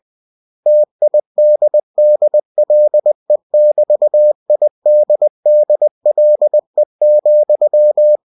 Morse code, heard as a keyed tone, about 20 wpm, 600 Hz. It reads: TIDDLE<BT>IDDLE,
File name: cw-books / winnie-the-pooh / winnie-20wpm-page-0013